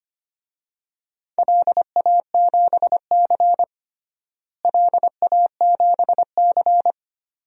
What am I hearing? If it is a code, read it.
Morse code, heard as a keyed tone, about 25 wpm, 700 Hz. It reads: LA7C LA7C
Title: DXCC-100 Call Signs - 4 Characters - Repeated 2x - 25wpm